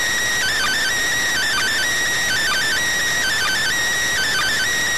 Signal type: Emergency Alert System (SAME) tones

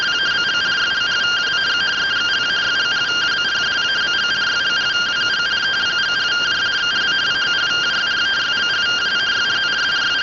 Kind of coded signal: two-tone teleprinter signal

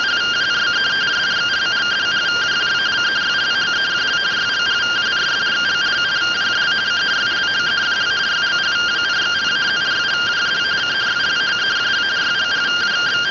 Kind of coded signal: two-tone teleprinter signal

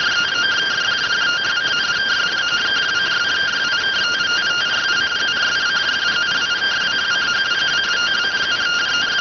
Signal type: two-tone teleprinter signal

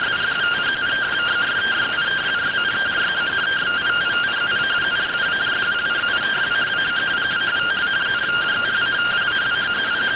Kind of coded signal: two-tone teleprinter signal

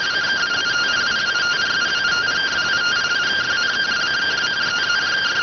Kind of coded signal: two-tone teleprinter signal

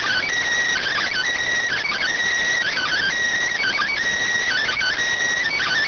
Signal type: Emergency Alert System (SAME) tones

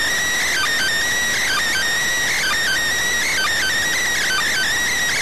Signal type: Emergency Alert System (SAME) tones

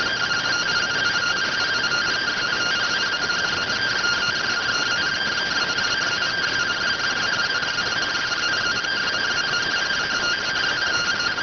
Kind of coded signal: two-tone teleprinter signal

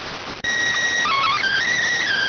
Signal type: SSTV (slow-scan television)

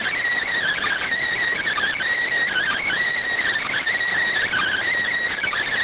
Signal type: Emergency Alert System (SAME) tones

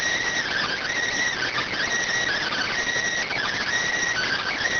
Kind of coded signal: Emergency Alert System (SAME) tones

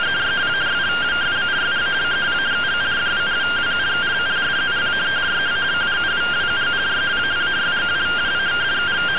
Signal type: two-tone teleprinter signal

